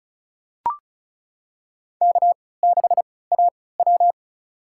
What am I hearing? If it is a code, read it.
Morse code, heard as a keyed tone, about 36 wpm, spaced as for 20 wpm, 700 Hz. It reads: K6AW
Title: USA Call Signs - 4 Characters - 3x Character Spacing - 35wpm